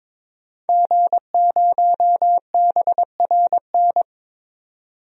Morse code, heard as a keyed tone, about 22 wpm, 700 Hz. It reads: G0BRN
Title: DXCC-100 Call Signs - 5 Characters - 22wpm